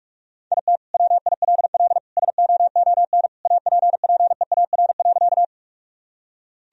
Morse code, heard as a keyed tone, about 46 wpm, 700 Hz. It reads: IT WILL SOON APPEAR.
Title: Sentences from Top 1000 Words - Review Part 2 - 45wpm